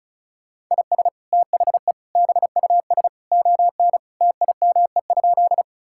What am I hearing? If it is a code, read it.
Morse code, heard as a keyed tone, about 36 wpm, 700 Hz. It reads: IS THE BUS ON TIME?
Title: Sentences from Top 1000 Words - Review Part 6 - 35wpm